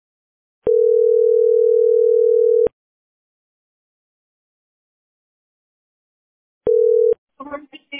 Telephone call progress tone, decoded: ringback tone